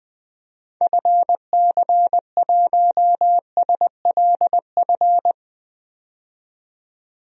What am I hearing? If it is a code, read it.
Morse code, heard as a keyed tone, about 21 wpm, 700 Hz. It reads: FC1SLF
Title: DXCC-100 Call Signs - 6 Characters - 20wpm